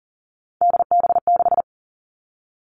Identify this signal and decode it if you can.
Morse code, heard as a keyed tone, about 42 wpm, 700 Hz. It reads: DB6
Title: Three Challenging Letter-Number Combinations 40wpm